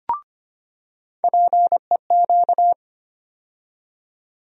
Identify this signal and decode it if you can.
Morse code, heard as a keyed tone, about 25 wpm, 700 Hz. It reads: PEQ
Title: USA Call Sign Postfixes - 3 Letters 25wpm